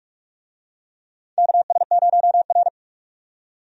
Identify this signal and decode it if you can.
Morse code, heard as a keyed tone, about 44 wpm, 700 Hz. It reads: KS0R
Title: USA Call Signs - 4 Characters - 45wpm